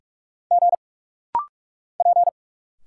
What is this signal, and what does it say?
Morse code, heard as a keyed tone, about 45 wpm, 700 Hz. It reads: G P
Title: Single Letters - Rapid-Fire 45wpm